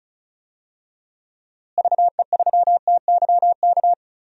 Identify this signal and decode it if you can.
Morse code, heard as a keyed tone, about 34 wpm, 700 Hz. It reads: VE3TYK
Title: DXCC-100 Call Signs - 6 Characters - Repeated 3x - 35wpm